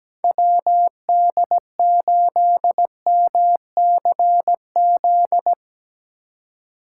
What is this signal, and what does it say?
Morse code, heard as a keyed tone, about 17 wpm, 700 Hz. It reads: WD8MCZ
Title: USA Call Signs - 6 Characters - 2x Character Spacing - 17wpm